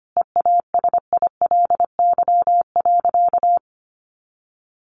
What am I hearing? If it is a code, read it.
Morse code, heard as a keyed tone, about 25 wpm, 700 Hz. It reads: EASILY.
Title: Sentences from Top 900 Words 25wpm